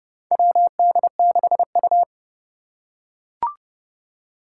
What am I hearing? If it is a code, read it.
Morse code, heard as a keyed tone, about 30 wpm, 700 Hz. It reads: WD6U
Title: USA Call Signs - 4 Characters - 2x Character Spacing - 30wpm